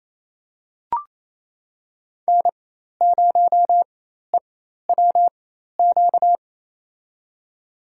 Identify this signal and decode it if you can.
Morse code, heard as a keyed tone, about 28 wpm, spaced as for 13 wpm, 700 Hz. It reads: N0EWQ
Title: USA Call Signs - 5 Characters - 4x Character Spacing - 28wpm